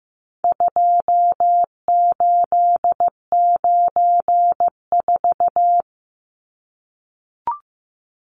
Morse code, heard as a keyed tone, about 15 wpm, 700 Hz. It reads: 2894
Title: Numbers - 4-Digit 15wpm